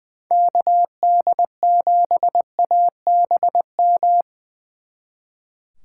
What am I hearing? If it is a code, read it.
Morse code, heard as a keyed tone, about 23 wpm, 700 Hz. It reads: KD7ABM
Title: Speed Racing - USA 2x3 Call Signs 20wpm